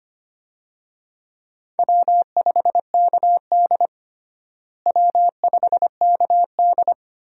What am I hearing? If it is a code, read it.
Morse code, heard as a keyed tone, about 25 wpm, 700 Hz. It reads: W5KD W5KD
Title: USA Call Signs - 4 Characters - Repeated 3x - 25wpm